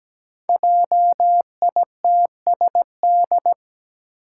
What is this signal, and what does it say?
Morse code, heard as a keyed tone, about 17 wpm, 700 Hz. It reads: JITSD
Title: Five Challenging Letter-Number Combinations 17wpm